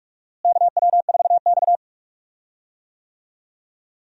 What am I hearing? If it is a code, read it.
Morse code, heard as a keyed tone, about 46 wpm, 700 Hz. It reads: KW4X
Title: USA Call Signs - 4 Characters - Repeated 2x - 45wpm